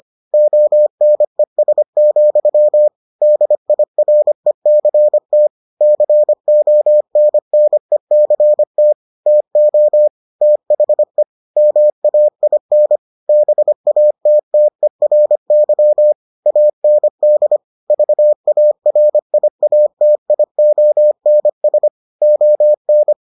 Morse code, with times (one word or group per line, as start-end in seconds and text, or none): 0.3-2.9 ONES,
3.2-5.5 DIRECT
5.8-8.9 CONNECT
9.3-10.1 TO
10.4-11.2 THE
11.6-13.0 MAIN
13.3-16.1 BATTERY
16.5-17.6 AND
17.9-21.9 VARIATIONS
22.2-23.1 ON